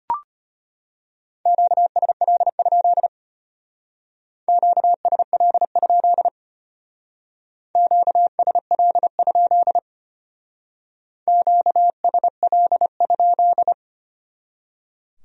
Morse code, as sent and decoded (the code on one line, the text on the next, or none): --.- ... .-.. ..--.. / --.- ... .-.. ..--.. / --.- ... .-.. ..--.. / --.- ... .-.. ..--..
QSL? QSL? QSL? QSL?